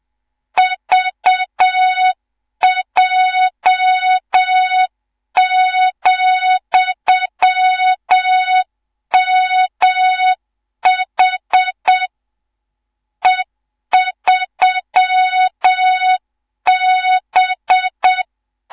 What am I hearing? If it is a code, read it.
Morse code, heard as a keyed tone, about 7 wpm, 3700 Hz, with noise. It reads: VJ,MH E3B
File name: Morze-HA2MN / hang / 18 IRASJELEK2